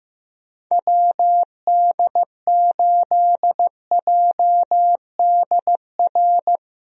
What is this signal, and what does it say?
Morse code, heard as a keyed tone, about 15 wpm, 700 Hz. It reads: WD8JDR
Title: USA Call Signs - 6 Characters - Repeated 3x - 15wpm